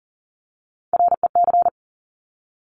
Morse code, heard as a keyed tone, about 40 wpm, 700 Hz. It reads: REC
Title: Top Three Letter Combinations in Words 40wpm